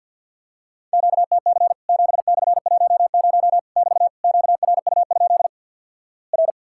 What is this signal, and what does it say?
Morse code, heard as a keyed tone, about 50 wpm, 700 Hz. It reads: QTC 7/10 <BT> QRU? R
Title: Worked All Europe DX Contest - QTC Exchange with Repeats - 50wpm